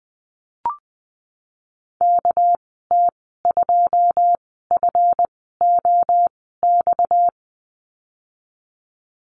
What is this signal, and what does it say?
Morse code, heard as a keyed tone, about 20 wpm, spaced as for 14 wpm, 700 Hz. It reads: KT2FOX
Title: 245 Lesson 39g - Letter z - Call Signs 20wpm